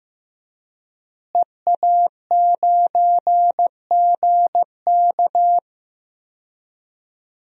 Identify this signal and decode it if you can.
Morse code, heard as a keyed tone, about 15 wpm, 700 Hz. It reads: EA9GK
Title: DXCC-100 Call Signs - 5 Characters - Repeated 2x - 15wpm